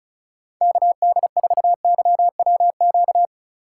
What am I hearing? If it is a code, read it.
Morse code, heard as a keyed tone, about 34 wpm, 700 Hz. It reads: KD4YWQ
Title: USA Call Signs - 6 Characters - 3x Character Spacing - 35wpm